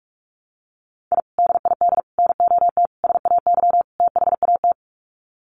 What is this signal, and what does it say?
Morse code, heard as a keyed tone, about 44 wpm, 700 Hz. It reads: I DID NOT SAY THAT
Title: Sets of 5 Words Encore2 45wpm